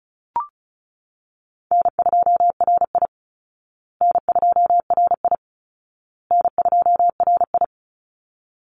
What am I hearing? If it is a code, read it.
Morse code, heard as a keyed tone, about 34 wpm, 700 Hz. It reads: N2RI N2RI N2RI
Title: Call Signs Repeated 2x - 35wpm